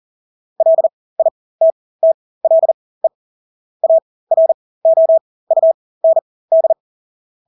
Morse code, heard as a keyed tone, about 40 wpm, spaced as for 20 wpm, 650 Hz. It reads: LITTLE AROUND
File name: Engelske ord 40-20 wpm